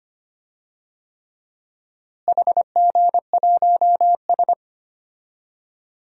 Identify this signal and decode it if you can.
Morse code, heard as a keyed tone, about 25 wpm, 700 Hz. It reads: HG1S
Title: DXCC-100 Call Signs - 4 Characters - Repeated 3x - 25wpm